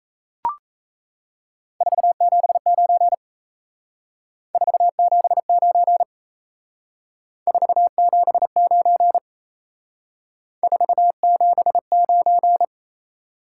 Morse code, heard as a keyed tone, about 36 wpm, 700 Hz. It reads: 479 479 479 479